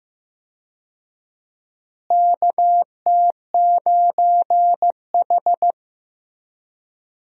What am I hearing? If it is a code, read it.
Morse code, heard as a keyed tone, about 15 wpm, 700 Hz. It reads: KT9H
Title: USA Call Signs - 4 Characters - Repeated 3x - 15wpm